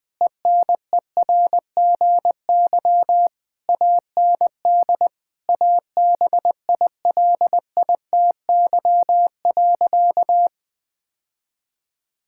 Morse code, heard as a keyed tone, about 21 wpm, 700 Hz. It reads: ENERGY AND ABILITY.
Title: Sentences from Top 1000 Words - Review Part 6 - 20wpm